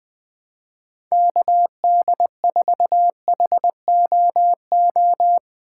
Morse code, heard as a keyed tone, about 20 wpm, 700 Hz. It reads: KD4HOO
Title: USA Call Signs - 6 Characters - 8x Character Spacing - 20wpm